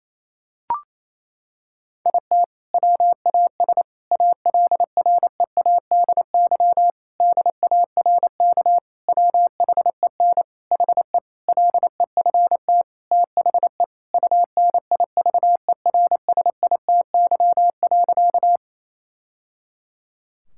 Morse code, heard as a keyed tone, about 28 wpm, 700 Hz. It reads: IT WAS ALREADY DARK WHEN HE LEFT THE UNIVERSITY.
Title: Sentences from Top 1000 Words - Review Part 5 - 28wpm